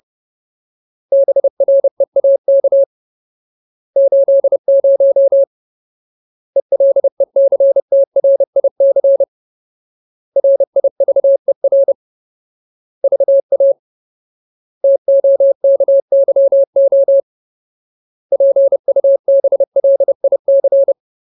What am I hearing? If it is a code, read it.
Morse code, heard as a keyed tone, about 30 wpm, 550 Hz. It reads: BREAK 80 ELECTRIC RIVER VA TOKYO PUBLIC
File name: Ultimate Word List / 30WPM Ultimate